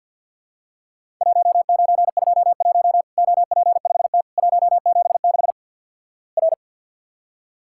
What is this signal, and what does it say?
Morse code, heard as a keyed tone, about 50 wpm, 700 Hz. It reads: 1921 OP5T 176 R
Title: Worked All Europe DX Contest - QTC Exchange - 50wpm